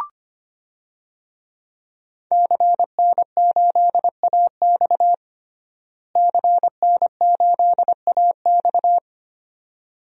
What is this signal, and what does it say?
Morse code, heard as a keyed tone, about 25 wpm, 700 Hz. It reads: CN8AX CN8AX
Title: DXCC-100 Call Signs - 5 Characters - Repeated 2x - 25wpm